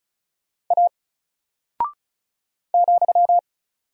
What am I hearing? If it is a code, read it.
Morse code, heard as a keyed tone, about 36 wpm, 700 Hz. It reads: A ,
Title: Single Character (Letter:Number:Punctuation:Prosign) Alt 35wpm